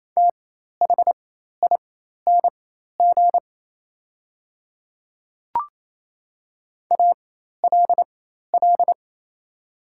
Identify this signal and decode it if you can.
Morse code, heard as a keyed tone, about 27 wpm, spaced as for 13 wpm, 700 Hz. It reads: THING ALL
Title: 4x Character Spacing - Top 200 Words - 28wpm